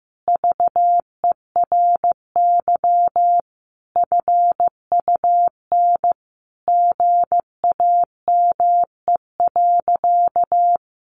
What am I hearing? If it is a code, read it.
Morse code, heard as a keyed tone, about 14 wpm, 700 Hz. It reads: VERY FUN GAME.
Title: Additional Easy Phrases 15wpm